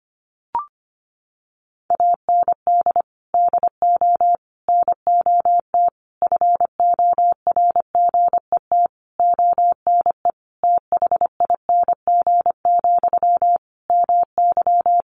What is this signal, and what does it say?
Morse code, heard as a keyed tone, about 25 wpm, 700 Hz. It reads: AND DO NOT FORGET ONE THING, MY
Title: Sentences from Top 1200 Words 25wpm